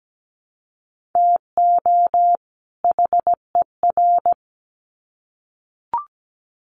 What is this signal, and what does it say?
Morse code, heard as a keyed tone, about 17 wpm, 700 Hz. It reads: TO HER
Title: Sets of 2 Words 17wpm